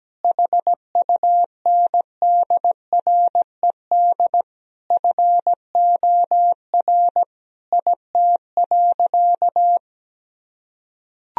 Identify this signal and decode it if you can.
Morse code, heard as a keyed tone, about 17 wpm, 700 Hz. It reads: HUNDRED FOR IT.
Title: Sentences from Top 1000 Words - Review Part 2 - 17wpm